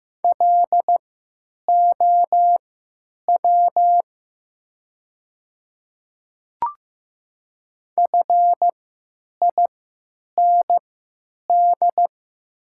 Morse code, heard as a keyed tone, about 15 wpm, spaced as for 9 wpm, 700 Hz. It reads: LOW FIND
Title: 3x Character Spacing - Top 200 Words - 15wpm